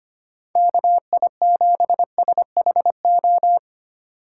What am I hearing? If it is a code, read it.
Morse code, heard as a keyed tone, about 25 wpm, 700 Hz. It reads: KI7SHO
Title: USA Call Signs - 6 Characters - 6x Character Spacing - 25wpm